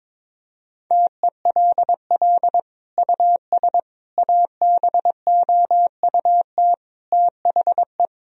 Morse code, heard as a keyed tone, about 22 wpm, 700 Hz. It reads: TELL US ABOUT THE
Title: Sentences from Top 1000 Words - Review Part 4 - 22wpm